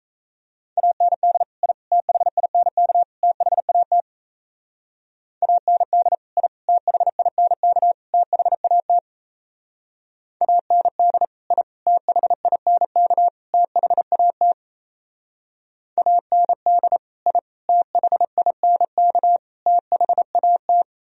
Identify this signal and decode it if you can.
Morse code, heard as a keyed tone, about 36 wpm, 700 Hz. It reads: AND I THINK THAT AND I THINK THAT AND I THINK THAT AND I THINK THAT